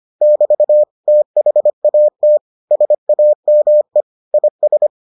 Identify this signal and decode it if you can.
Morse code, heard as a keyed tone, about 25 wpm, 600 Hz. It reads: <BT> THAT SAME IS